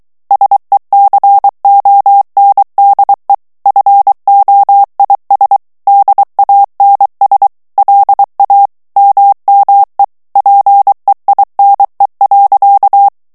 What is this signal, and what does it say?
Morse code, heard as a keyed tone, about 23 wpm, 800 Hz. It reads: SECONDE FOIS DANS LA MME PEINE.
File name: FRANCAIS TEXTE CLAIR 2 VITESSE 22 MOTS MINUTES